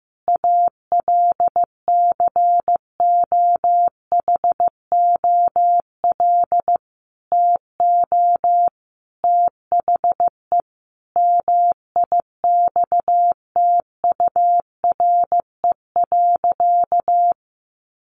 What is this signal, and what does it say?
Morse code, heard as a keyed tone, about 15 wpm, 700 Hz. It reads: ALCOHOL TO THE MIXTURE.
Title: Sentences from Top 1500 Words 15wpm